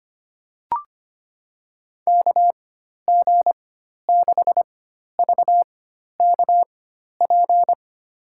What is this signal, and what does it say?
Morse code, heard as a keyed tone, about 25 wpm, spaced as for 12 wpm, 700 Hz. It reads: KG6VKP
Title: USA Call Signs - 6 Characters - 4x Character Spacing - 25wpm